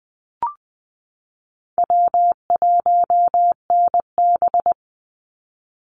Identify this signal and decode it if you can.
Morse code, heard as a keyed tone, about 20 wpm, 700 Hz. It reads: W1NB
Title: USA Call Signs 20wpm